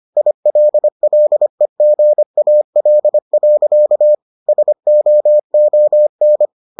Morse code, heard as a keyed tone, about 25 wpm, 600 Hz. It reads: ILLEGAL. SOON